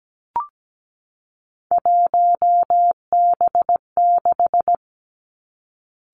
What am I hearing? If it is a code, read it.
Morse code, heard as a keyed tone, about 17 wpm, 700 Hz. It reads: 1B6
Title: Three Challenging Letter-Number Combinations 17wpm